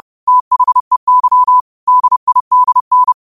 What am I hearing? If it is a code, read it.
Morse code, heard as a keyed tone, about 30 wpm, 1000 Hz. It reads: THEY DIDN
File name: qotdfiles / 30wpm